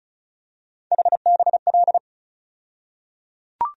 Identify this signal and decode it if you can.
Morse code, heard as a keyed tone, about 34 wpm, 700 Hz. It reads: HBL
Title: USA Call Sign Postfixes - 3 Letters 35wpm